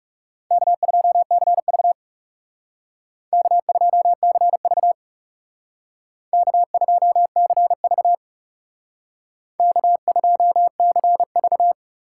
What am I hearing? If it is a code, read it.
Morse code, heard as a keyed tone, about 35 wpm, 700 Hz. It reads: K2CV K2CV K2CV K2CV
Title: Speed Racing - USA 1x2 Call Signs 30wpm